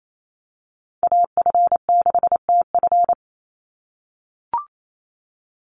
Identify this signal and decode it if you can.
Morse code, heard as a keyed tone, about 28 wpm, 700 Hz. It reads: AF6TF
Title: USA Call Signs 28wpm